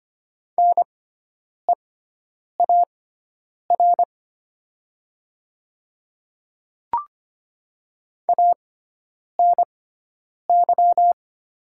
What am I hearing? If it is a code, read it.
Morse code, heard as a keyed tone, about 25 wpm, spaced as for 9 wpm, 700 Hz. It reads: NEAR ANY
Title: 6x Character Spacing - Top 300 Words - 25wpm